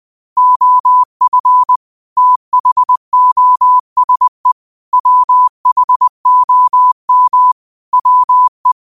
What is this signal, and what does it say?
Morse code, heard as a keyed tone, about 20 wpm, 1000 Hz. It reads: OF THOSE WHOM WE
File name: qotdfiles / 20wpm